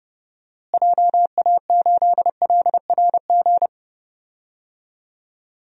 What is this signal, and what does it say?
Morse code, heard as a keyed tone, about 30 wpm, 700 Hz. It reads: JA8LRG
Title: DXCC-100 Call Signs - 6 Characters - Repeated 3x - 30wpm